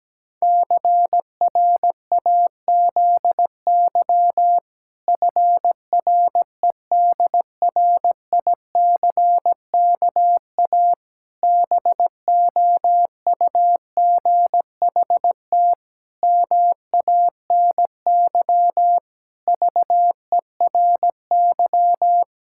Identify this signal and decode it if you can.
Morse code, heard as a keyed tone, about 17 wpm, 700 Hz. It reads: CRAZY FREDRICKA BOUGHT MANY VERY
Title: Pangram Challenge v2 - 17wpm